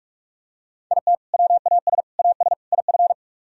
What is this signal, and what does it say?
Morse code, heard as a keyed tone, about 44 wpm, 700 Hz. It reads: IT WAS AS IF